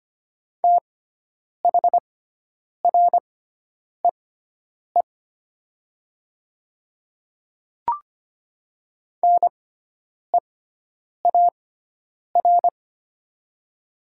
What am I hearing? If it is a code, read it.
Morse code, heard as a keyed tone, about 25 wpm, spaced as for 9 wpm, 700 Hz. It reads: THREE NEAR
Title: 6x Character Spacing - Top 500 Words - 25wpm